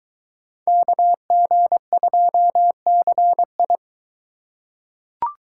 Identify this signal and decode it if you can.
Morse code, heard as a keyed tone, about 23 wpm, 700 Hz. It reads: KG2CI